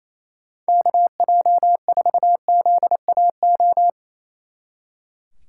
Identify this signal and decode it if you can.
Morse code, heard as a keyed tone, about 31 wpm, 700 Hz. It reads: KJ4ZAO
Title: Speed Racing - USA 2x3 Call Signs 28wpm